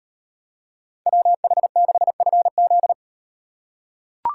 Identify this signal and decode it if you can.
Morse code, heard as a keyed tone, about 38 wpm, 700 Hz. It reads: WH6FZ